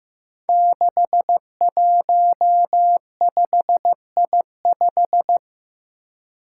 Morse code, heard as a keyed tone, about 16 wpm, 700 Hz. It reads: 615I5
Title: Five Challenging Letter-Number Combinations 15wpm